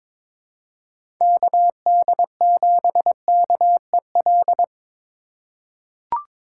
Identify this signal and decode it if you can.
Morse code, heard as a keyed tone, about 22 wpm, 700 Hz. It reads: KD7KEL
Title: USA Call Signs - 3x Character Spacing - 22wpm